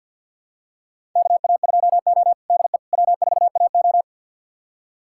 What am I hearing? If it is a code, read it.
Morse code, heard as a keyed tone, about 51 wpm, 700 Hz. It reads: KA2O DE W4AO
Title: USA Callsign DE Callsign - 4 Characters - 50wpm